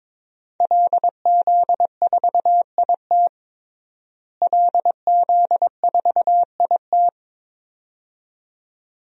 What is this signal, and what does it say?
Morse code, heard as a keyed tone, about 22 wpm, 700 Hz. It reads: LZ4IT LZ4IT
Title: DXCC-100 Call Signs - 5 Characters - Repeated 2x - 22wpm